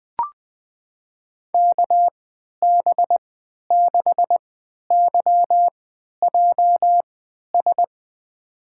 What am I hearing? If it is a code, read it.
Morse code, heard as a keyed tone, about 20 wpm, spaced as for 11 wpm, 700 Hz. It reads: KB6YJS
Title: USA Call Signs - 3x Character Spacing - 20wpm